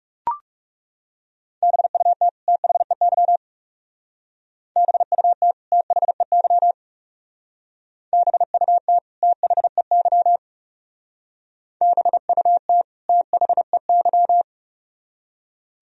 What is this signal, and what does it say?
Morse code, heard as a keyed tone, about 36 wpm, 700 Hz. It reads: BUT THEY BUT THEY BUT THEY BUT THEY
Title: Speed-Racing - Sets of 2 Words 30wpm